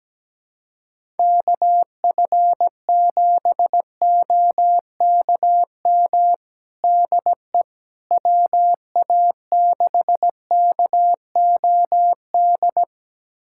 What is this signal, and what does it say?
Morse code, heard as a keyed tone, about 17 wpm, 700 Hz. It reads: KF7OKM DE WA6KOD
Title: USA Callsign DE Callsign - 6 Characters - 17wpm